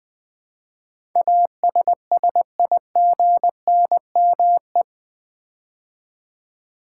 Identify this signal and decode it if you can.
Morse code, heard as a keyed tone, about 20 wpm, 700 Hz. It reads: ASSIGNME
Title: Build A Word - 4 Letters At A Time - 10-Letter Words - 20wpm